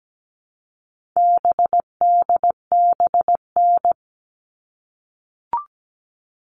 Morse code, heard as a keyed tone, about 17 wpm, 700 Hz. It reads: BDBN
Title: All Cut Numbers - 4-Digit 17wpm